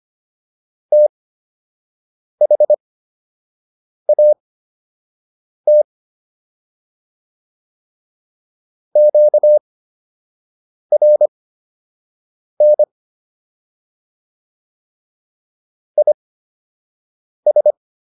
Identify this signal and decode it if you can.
Morse code, heard as a keyed tone, about 25 wpm, spaced as for 6 wpm, 600 Hz. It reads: THAT QRN IS